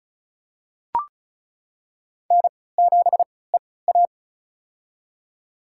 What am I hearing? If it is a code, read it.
Morse code, heard as a keyed tone, about 36 wpm, spaced as for 20 wpm, 700 Hz. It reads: N7EA
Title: USA Call Signs - 3x Character Spacing - 35wpm